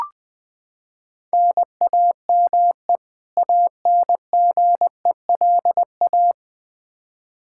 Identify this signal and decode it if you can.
Morse code, heard as a keyed tone, about 20 wpm, 700 Hz. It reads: NAME ANGELA
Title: Practice QSO - Exchange 1 - Beginning - 20wpm